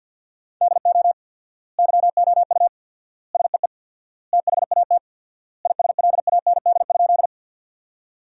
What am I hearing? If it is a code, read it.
Morse code, heard as a keyed tone, about 51 wpm, 700 Hz. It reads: DO YOU SEE THAT ISLAND?
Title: 4x Word-Spacing - Sentences from Top 500 Words - 50wpm